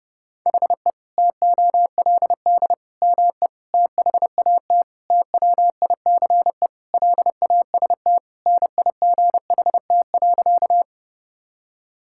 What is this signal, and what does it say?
Morse code, heard as a keyed tone, about 30 wpm, 700 Hz. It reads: HE TOLD ME THAT TWICE LAST NIGHT.
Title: Sentences from Top 1600 Words 30wpm